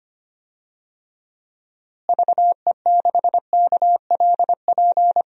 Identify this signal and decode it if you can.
Morse code, heard as a keyed tone, about 25 wpm, 700 Hz. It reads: VE6KLP
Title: DXCC-100 Call Signs - 6 Characters - Repeated 2x - 25wpm